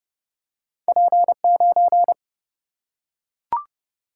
Morse code, heard as a keyed tone, about 30 wpm, 700 Hz. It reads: P9
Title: Single-Letter Single-Number 30wpm